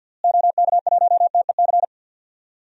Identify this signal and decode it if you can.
Morse code, heard as a keyed tone, about 49 wpm, 700 Hz. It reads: OK1TEC
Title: DXCC-100 Call Signs - 6 Characters - Repeated 3x - 50wpm